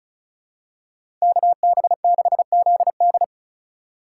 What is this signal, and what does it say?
Morse code, heard as a keyed tone, about 34 wpm, 700 Hz. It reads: KB6ZD
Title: USA Call Signs - 5 Characters - 3x Character Spacing - 35wpm